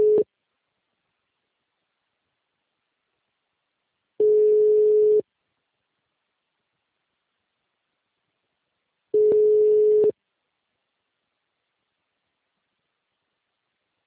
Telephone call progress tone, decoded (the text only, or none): ringback tone